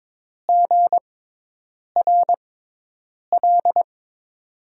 Morse code, heard as a keyed tone, about 22 wpm, 700 Hz. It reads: G R L
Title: USA Call Signs - 6x Character Spacing - 22wpm